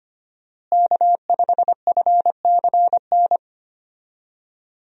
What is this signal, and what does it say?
Morse code, heard as a keyed tone, about 25 wpm, 700 Hz. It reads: K5FCN
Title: USA Call Signs - 5 Characters - Repeated 3x - 25wpm